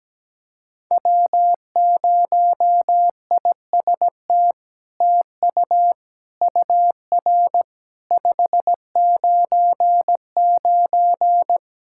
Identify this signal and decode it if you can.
Morse code, heard as a keyed tone, about 17 wpm, 700 Hz. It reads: W0IST TU UR 599
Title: SOTA QSO Practice 17wpm